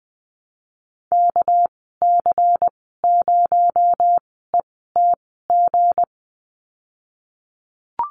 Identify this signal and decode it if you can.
Morse code, heard as a keyed tone, about 20 wpm, spaced as for 15 wpm, 700 Hz. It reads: KC0ETG
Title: 219 Lesson 35g - Number 0 - Call Signs 20wpm